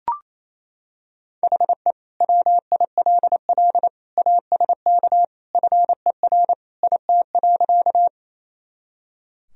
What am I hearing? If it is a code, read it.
Morse code, heard as a keyed tone, about 28 wpm, 700 Hz. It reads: HE WILL ASK FER IT.
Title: Sentences from Top 200 Words and Abbreviations 28wpm